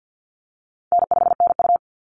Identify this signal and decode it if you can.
Morse code, heard as a keyed tone, about 50 wpm, 700 Hz. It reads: N5NU
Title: Call Signs - 50wpm